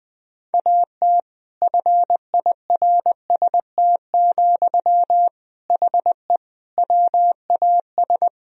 Morse code, heard as a keyed tone, about 20 wpm, 700 Hz. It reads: AT FIRST, HE WAS
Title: Sentences from Top 1000 Words - Review Part 4 - 20wpm